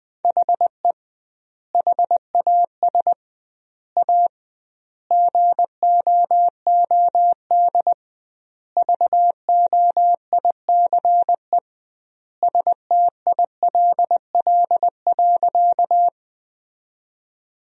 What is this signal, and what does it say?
Morse code, heard as a keyed tone, about 20 wpm, 700 Hz. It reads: HE HAS A GOOD VOICE STILL.
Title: 2x Word-Spacing - Sentences from Top 500 Words - 20wpm